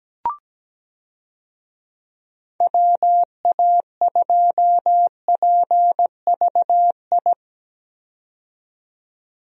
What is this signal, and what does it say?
Morse code, heard as a keyed tone, about 17 wpm, 700 Hz. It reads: WA2PVI
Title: USA Call Signs - 6 Characters - 17wpm